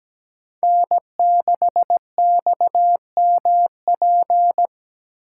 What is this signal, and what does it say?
Morse code, heard as a keyed tone, about 17 wpm, 700 Hz. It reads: N6XMP
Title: USA Call Signs - 5 Characters - 2x Character Spacing - 17wpm